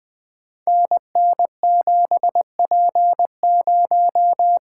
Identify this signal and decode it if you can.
Morse code, heard as a keyed tone, about 20 wpm, 700 Hz. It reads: NN7P0
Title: Code Groups - 5 Letter:Numbers - 20wpm